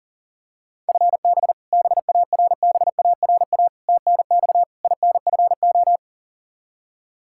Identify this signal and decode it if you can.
Morse code, heard as a keyed tone, about 40 wpm, 700 Hz. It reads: FB BARBARA TNX INFO
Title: Practice QSO - Exchange 3 - More Info - 40wpm